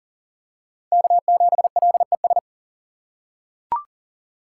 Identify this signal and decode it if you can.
Morse code, heard as a keyed tone, about 37 wpm, 700 Hz. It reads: K7LES
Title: USA Call Signs - 5 Characters - 2x Character Spacing - 40wpm